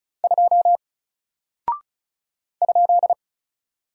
Morse code, heard as a keyed tone, about 36 wpm, 700 Hz. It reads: 2 ?
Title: Single Character (Letter:Number:Punctuation:Prosign) Alt 35wpm